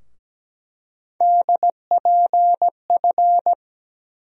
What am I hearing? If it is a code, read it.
Morse code, heard as a keyed tone, about 17 wpm, 700 Hz. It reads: DPF
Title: USA Call Sign Postfixes - 3 Letters 17wpm